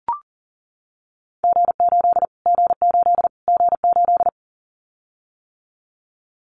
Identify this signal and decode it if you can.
Morse code, heard as a keyed tone, about 40 wpm, 700 Hz. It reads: G8 G8 G8
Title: Callsign Prefixes for Top 25 DXCC - Repeated 3x - 40wpm